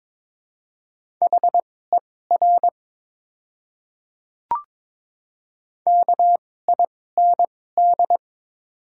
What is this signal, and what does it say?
Morse code, heard as a keyed tone, about 22 wpm, spaced as for 16 wpm, 700 Hz. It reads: HER KIND
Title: 2x Character Spacing - Top 200 Words - 22wpm